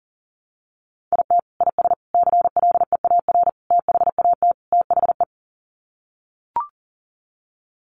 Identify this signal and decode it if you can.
Morse code, heard as a keyed tone, about 37 wpm, 700 Hz. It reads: IT IS CLEAR THAT THE
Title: Sets of 5 Words Encore 40wpm